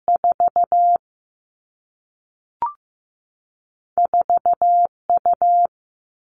Morse code, heard as a keyed tone, about 15 wpm, 700 Hz. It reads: 4 4U
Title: All Cut Numbers - 2-Digit 15wpm